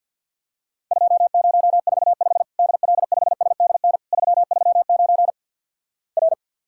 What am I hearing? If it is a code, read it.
Morse code, heard as a keyed tone, about 51 wpm, 700 Hz. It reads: 2045 DL5SDN 339 R
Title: Worked All Europe DX Contest - QTC Exchange with Repeats - 50wpm